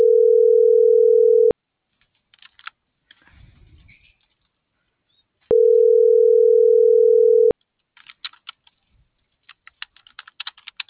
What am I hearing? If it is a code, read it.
Ringback tone (North America), 440 + 480 Hz.